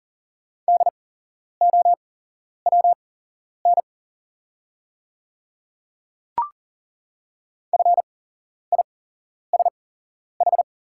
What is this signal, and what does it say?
Morse code, heard as a keyed tone, about 40 wpm, spaced as for 11 wpm, 690 Hz. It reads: DOWN FISH
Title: 8x Character Spacing - Top 300 Words - 40wpm